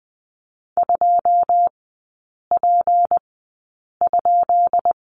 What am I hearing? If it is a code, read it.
Morse code, heard as a keyed tone, about 20 wpm, spaced as for 8 wpm, 700 Hz. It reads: 2P?
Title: 172 Lesson 28e - Number 9 - Three Characters 20wpm